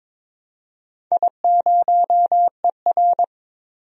Morse code, heard as a keyed tone, about 22 wpm, 700 Hz. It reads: I0ER
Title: DXCC-100 Call Signs - 4 Characters - 22wpm